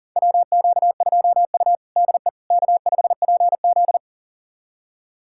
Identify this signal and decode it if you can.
Morse code, heard as a keyed tone, about 40 wpm, 700 Hz. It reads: WQ2U DE K5PZ
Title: USA Callsign DE Callsign - 4 Characters - 40wpm